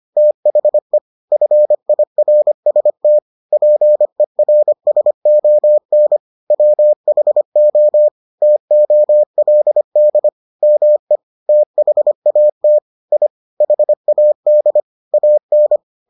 Morse code, heard as a keyed tone, about 25 wpm, 600 Hz. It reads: THE FIRST PERSON WHO TOLD ME THAT I HAD AN